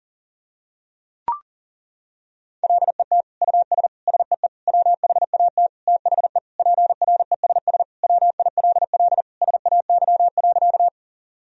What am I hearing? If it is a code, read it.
Morse code, heard as a keyed tone, about 40 wpm, 700 Hz. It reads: LET US SEE WHAT THE PRESS WILL SAY.
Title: Sentences from Top 1000 Words - Review Part 1 - 40wpm